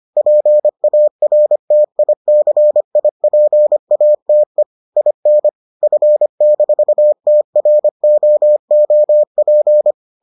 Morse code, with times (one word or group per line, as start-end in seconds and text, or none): 0.2-4.6 PARTICIPATE
5.0-5.5 IN
5.8-9.9 F-TROOP